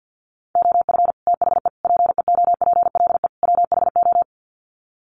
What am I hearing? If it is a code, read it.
Morse code, heard as a keyed tone, about 50 wpm, 700 Hz. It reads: OF THE PEOPLE WHO
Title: Sets of 4 Words 50wpm